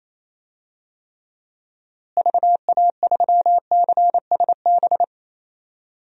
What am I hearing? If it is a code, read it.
Morse code, heard as a keyed tone, about 28 wpm, 700 Hz. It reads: VA3CSB
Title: DXCC-100 Call Signs - 6 Characters - Repeated 3x - 28wpm